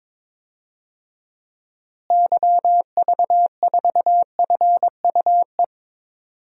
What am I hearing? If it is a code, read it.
Morse code, heard as a keyed tone, about 22 wpm, 700 Hz. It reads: YV4FUE
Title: DXCC-100 Call Signs - 6 Characters - Repeated 2x - 22wpm